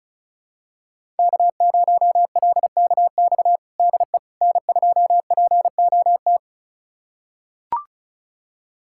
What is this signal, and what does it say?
Morse code, heard as a keyed tone, about 36 wpm, 700 Hz. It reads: K0LKX DE N2POT
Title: USA Callsign DE Callsign - 5 Characters - 35wpm